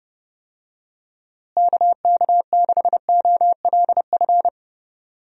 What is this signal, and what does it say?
Morse code, heard as a keyed tone, about 32 wpm, 700 Hz. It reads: KK6OLF
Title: USA Call Signs - 6 Characters - Repeated 3x - 30wpm